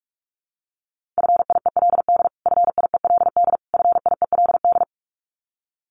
Morse code, heard as a keyed tone, about 44 wpm, 700 Hz. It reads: FIELD FIELD FIELD
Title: Top 500 Words Repeated 45wpm